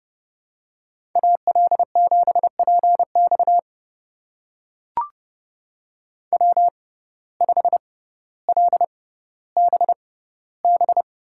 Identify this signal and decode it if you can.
Morse code, heard as a keyed tone, about 30 wpm, spaced as for 15 wpm, 700 Hz. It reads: AL7PX W5LBB
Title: USA Call Signs - 5 Characters - 6x Character Spacing - 30wpm